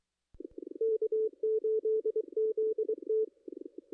Morse code, heard as a keyed tone, about 24 wpm, 440 Hz. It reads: K8ZT
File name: contests / ss-cw-k5tr-op-n5rz -1400